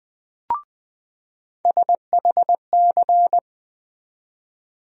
Morse code, heard as a keyed tone, about 20 wpm, 700 Hz. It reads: SHC
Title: USA Call Sign Postfixes - 3 Letters 20wpm